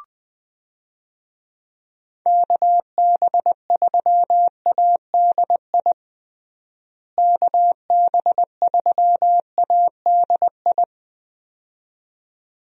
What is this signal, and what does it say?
Morse code, heard as a keyed tone, about 20 wpm, 700 Hz. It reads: KB3ADI KB3ADI
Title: USA Call Signs - 6 Characters - Repeated 2x - 20wpm